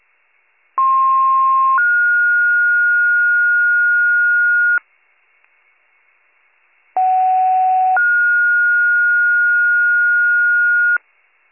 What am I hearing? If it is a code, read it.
Two-tone page, heard as a one-second tone, then a three-second tone, about 1060 then 1430 Hz, about 750 then 1430 Hz. It is heard from 0.8 to 11.0 s.